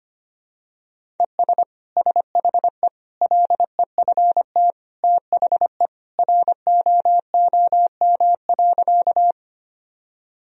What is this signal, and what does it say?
Morse code, heard as a keyed tone, about 25 wpm, 700 Hz. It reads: ES SHE LEFT THE ROOM.
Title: Sentences from Top 300 Words and Abbreviations 25wpm